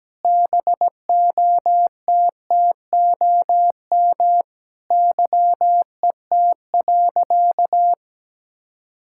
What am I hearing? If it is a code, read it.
Morse code, heard as a keyed tone, about 17 wpm, 700 Hz. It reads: BOTTOM YET.
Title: Sentences from Top 1000 Words - Review Part 6 - 17wpm